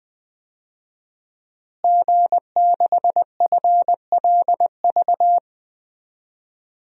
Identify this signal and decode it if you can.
Morse code, heard as a keyed tone, about 20 wpm, 700 Hz. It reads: G6FLV
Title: DXCC-100 Call Signs - 5 Characters - Repeated 3x - 20wpm